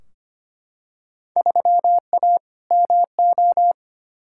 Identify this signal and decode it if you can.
Morse code, heard as a keyed tone, about 25 wpm, 700 Hz. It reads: 3A MO
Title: Field Day Exchanges - 25wpm